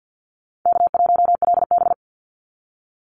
Morse code, heard as a keyed tone, about 50 wpm, 700 Hz. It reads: K1LB
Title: NA SOTA Activator Callsigns 50wpm